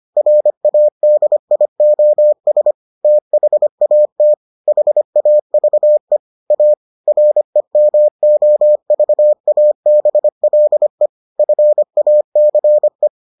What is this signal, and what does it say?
Morse code, heard as a keyed tone, about 25 wpm, 600 Hz. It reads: RADIOS THAT HAVE A REMOVABLE FACE